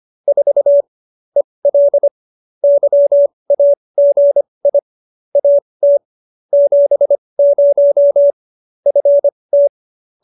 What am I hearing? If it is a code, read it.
Morse code, heard as a keyed tone, about 25 wpm, spaced as for 20 wpm, 570 Hz. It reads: 4 EL YAGI AT 70 FT
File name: QSO307 20